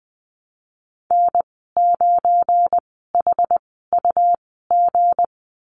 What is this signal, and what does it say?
Morse code, heard as a keyed tone, about 20 wpm, spaced as for 15 wpm, 700 Hz. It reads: N9HUG
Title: 231 Lesson 37g - Letter x - Call Signs 20wpm